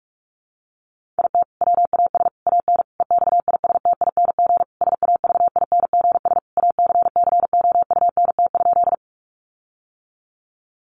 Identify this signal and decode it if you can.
Morse code, heard as a keyed tone, about 44 wpm, 700 Hz. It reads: IT WAS AN EXISTING SAVINGS ACCOUNT?
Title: Sentences from Top 1500 Words 45wpm